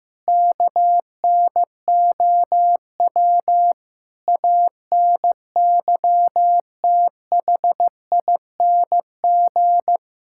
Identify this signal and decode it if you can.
Morse code, heard as a keyed tone, about 15 wpm, 700 Hz. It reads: KNOW ANYTHING
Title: Sentences from Top 1000 Words - Review Part 6 - 15wpm